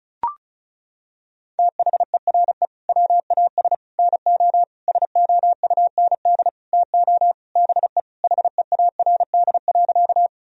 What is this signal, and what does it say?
Morse code, heard as a keyed tone, about 36 wpm, 700 Hz. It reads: THERE WAS NO SOUND TO BE HEARD.
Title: Sentences from Top 1000 Words - Review Part 1 - 35wpm